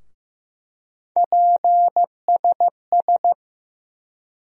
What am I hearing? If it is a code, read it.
Morse code, heard as a keyed tone, about 15 wpm, 700 Hz. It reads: PSS
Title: USA Call Sign Postfixes - 3 Letters 15wpm